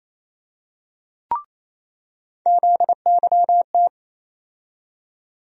Morse code, heard as a keyed tone, about 28 wpm, 700 Hz. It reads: ZYT
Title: USA Call Sign Postfixes - 3 Letters 28wpm